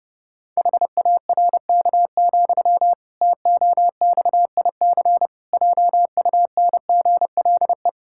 Morse code, heard as a keyed tone, about 30 wpm, 700 Hz. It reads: HARK, TOXIC JUNGLE
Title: Pangram Challenge v2 - 30wpm